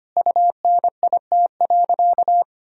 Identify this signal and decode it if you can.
Morse code, heard as a keyed tone, about 25 wpm, 700 Hz. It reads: UNIT.
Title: Sentences from Top 1000 Words - Review Part 2 - 25wpm